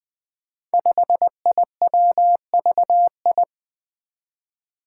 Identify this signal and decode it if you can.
Morse code, heard as a keyed tone, about 20 wpm, 700 Hz. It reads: 5IWVI
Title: Five Challenging Letter-Number Combinations 20wpm